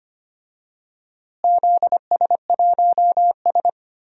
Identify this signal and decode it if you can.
Morse code, heard as a keyed tone, about 25 wpm, 700 Hz. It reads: ZS1S
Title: DXCC-100 Call Signs - 4 Characters - Repeated 2x - 25wpm